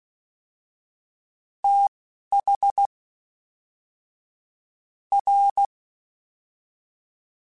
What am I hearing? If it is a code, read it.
Morse code, heard as a keyed tone, about 16 wpm, spaced as for 5 wpm, 800 Hz. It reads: THR
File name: TH / TH mp3s / Fast TH @10